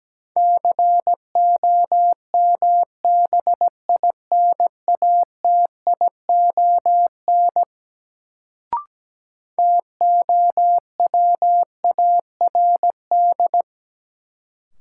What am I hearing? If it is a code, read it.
Morse code, heard as a keyed tone, about 17 wpm, 700 Hz. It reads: COMBINATION TOWARD
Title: Sentences from Top 1000 Words - Review Part 6 - New Word Drill - 17wpm